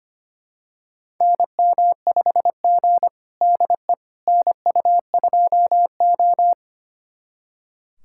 Morse code, heard as a keyed tone, about 25 wpm, 700 Hz. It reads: NM5G DE NU2O
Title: USA Callsign DE Callsign - 4 Characters - 25wpm